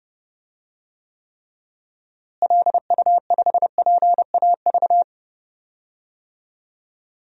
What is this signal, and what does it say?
Morse code, heard as a keyed tone, about 32 wpm, 700 Hz. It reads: LU5PAV
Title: DXCC-100 Call Signs - 6 Characters - Repeated 2x - 30wpm